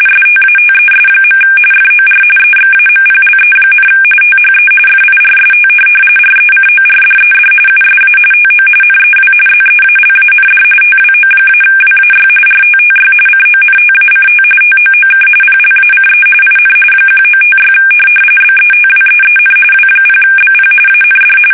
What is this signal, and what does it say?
RTTY, about 45.45 baud. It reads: STATES. FOR EXAMPLE, THE FIRST RECORDED USA TO NEW ZEALAND TWO-WAY RTTY CONTACT TOOK PLACE IN 1956 BETWEEN W0BP AND
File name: fsk850 / teletype-RTTY